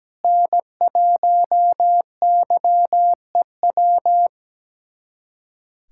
Morse code, heard as a keyed tone, about 17 wpm, 700 Hz. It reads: N1YEW
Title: USA Callsign DE Callsign 17wpm